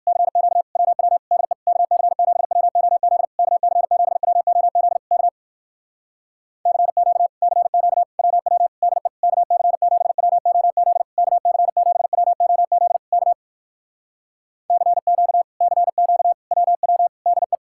Morse code, heard as a keyed tone, about 68 wpm, 700 Hz. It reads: CQ WW DE KC7WOZ KC7WOZ K CQ CQ WW DE KC7WOZ KC7WOZ K CQ CQ WW DE